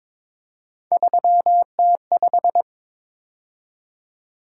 Morse code, heard as a keyed tone, about 26 wpm, 700 Hz. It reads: 3T5
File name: Speed-Racing - Three Letter-Number Combinations - 22wpm